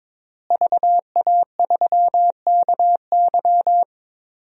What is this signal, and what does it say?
Morse code, heard as a keyed tone, about 22 wpm, 700 Hz. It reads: VA3KY
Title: DXCC-100 Call Signs - 5 Characters - Repeated 3x - 22wpm